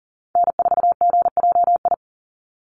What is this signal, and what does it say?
Morse code, heard as a keyed tone, about 42 wpm, 700 Hz. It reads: N4GJI
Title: USA Call Signs 40wpm